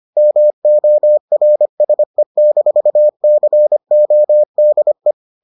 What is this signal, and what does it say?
Morse code, heard as a keyed tone, about 25 wpm, 600 Hz. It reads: MORSE-CODE